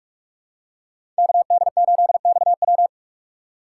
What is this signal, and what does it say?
Morse code, heard as a keyed tone, about 44 wpm, 700 Hz. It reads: KD8XW
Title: USA Call Signs - 5 Characters - 6x Character Spacing - 45wpm